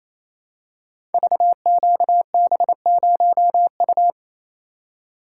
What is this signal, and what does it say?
Morse code, heard as a keyed tone, about 28 wpm, 700 Hz. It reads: VQB0U
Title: Code Groups - 5 Letter:Numbers - 28wpm